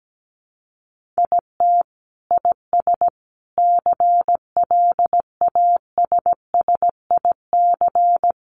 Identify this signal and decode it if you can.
Morse code, heard as a keyed tone, about 17 wpm, 700 Hz. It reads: IT IS CLASSIC
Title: Sentences from Top 1000 Words 17wpm